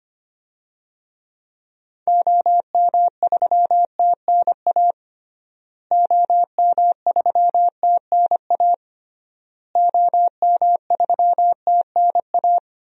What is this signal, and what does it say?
Morse code, heard as a keyed tone, about 25 wpm, 700 Hz. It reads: OM3TNA OM3TNA OM3TNA
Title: DXCC-100 Call Signs - 6 Characters - Repeated 3x - 25wpm